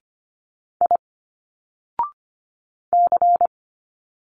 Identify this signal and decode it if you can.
Morse code, heard as a keyed tone, about 25 wpm, 700 Hz. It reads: I C
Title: Single Letters 25wpm